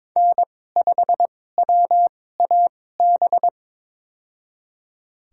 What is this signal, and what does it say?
Morse code, heard as a keyed tone, about 22 wpm, spaced as for 16 wpm, 700 Hz. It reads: N5WAB
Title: USA Call Signs - 5 Characters - 2x Character Spacing - 22wpm